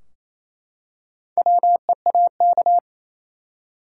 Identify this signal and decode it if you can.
Morse code, heard as a keyed tone, about 28 wpm, 700 Hz. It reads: WEAK
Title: Four Letter Words - 28wpm